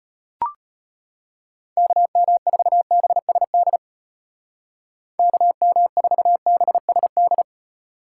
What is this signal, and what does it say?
Morse code, heard as a keyed tone, about 33 wpm, 700 Hz. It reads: KM4BSD KM4BSD